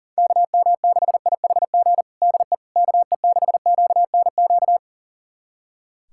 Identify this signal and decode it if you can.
Morse code, heard as a keyed tone, about 37 wpm, 700 Hz. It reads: KM6IHG DE KE6QNQ
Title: USA Callsign DE Callsign - 6 Characters - 40wpm